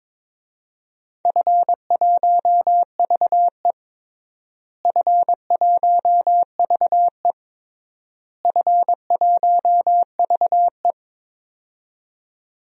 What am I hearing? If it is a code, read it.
Morse code, heard as a keyed tone, about 22 wpm, 700 Hz. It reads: F1VE F1VE F1VE
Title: DXCC-100 Call Signs - 4 Characters - Repeated 3x - 22wpm